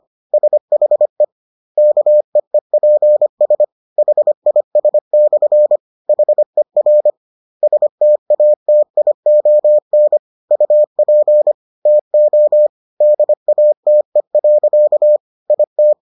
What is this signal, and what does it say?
Morse code, heard as a keyed tone, about 25 wpm, 600 Hz. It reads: SHE KEEPS HIS/HER STATION UP TO DATE. IT